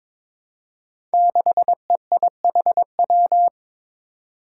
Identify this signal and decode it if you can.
Morse code, heard as a keyed tone, about 22 wpm, 700 Hz. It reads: 6EIHW
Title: Five Challenging Letter-Number Combinations 22wpm